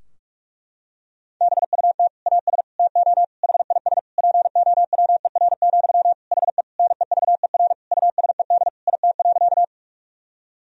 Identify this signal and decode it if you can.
Morse code, heard as a keyed tone, about 46 wpm, 700 Hz. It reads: BUT AS TO HIS POWER, HE NEVER USED IT.
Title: Sentences from Top 1000 Words - Review Part 2 - 45wpm